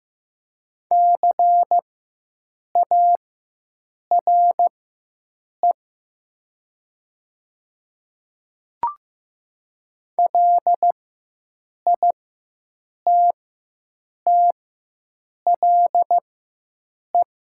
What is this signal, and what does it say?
Morse code, heard as a keyed tone, about 15 wpm, spaced as for 7 wpm, 700 Hz. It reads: CARE LITTLE
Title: 4x Character Spacing - Top 300 Words - 15wpm